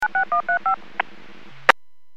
DTMF, heard as telephone touch tones, tones of about 110 ms. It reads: 96135
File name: WPS / 00295 19-41 WPS 451.25 WPS